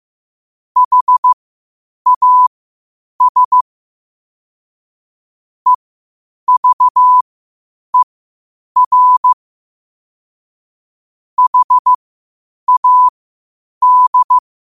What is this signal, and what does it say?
Morse code, heard as a keyed tone, about 15 wpm, spaced as for 8 wpm, 1000 Hz. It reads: HAS EVER HAD
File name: qotdfiles / 8wpm